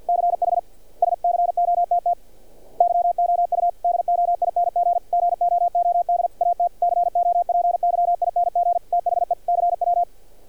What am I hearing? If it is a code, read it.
Morse code, heard as a keyed tone, about 46 wpm, 700 Hz. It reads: CH SCOTT YOU DOING GOOD AT COPYING THE CW
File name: archived / CRSnet